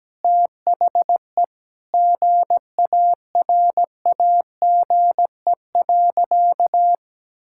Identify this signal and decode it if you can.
Morse code, heard as a keyed tone, about 17 wpm, 700 Hz. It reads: THE GARAGE.
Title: Sentences from Top 1600 Words 17wpm